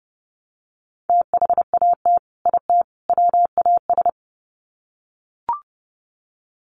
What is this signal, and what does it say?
Morse code, heard as a keyed tone, about 30 wpm, 700 Hz. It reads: THAT IT WAS
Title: Sets of 3 Words 30wpm